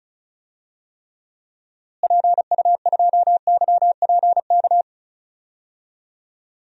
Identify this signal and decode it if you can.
Morse code, heard as a keyed tone, about 36 wpm, 700 Hz. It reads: PU2YPK
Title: DXCC-100 Call Signs - 6 Characters - Repeated 2x - 35wpm